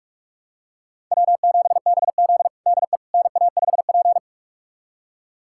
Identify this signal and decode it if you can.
Morse code, heard as a keyed tone, about 46 wpm, 690 Hz. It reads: W7BZ DE NA5P
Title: USA Callsign DE Callsign - 4 Characters - 45wpm